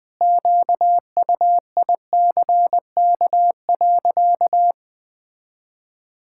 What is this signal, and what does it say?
Morse code, heard as a keyed tone, about 20 wpm, 700 Hz. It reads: QUICK.
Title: Sentences from Top 1600 Words 20wpm